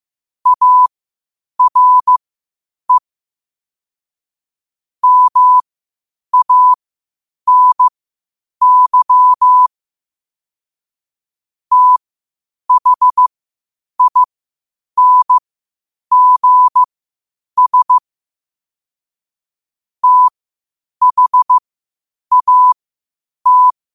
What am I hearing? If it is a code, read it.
Morse code, heard as a keyed tone, about 15 wpm, spaced as for 8 wpm, 1000 Hz. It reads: ARE MANY THINGS THAT
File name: qotdfiles / 8wpm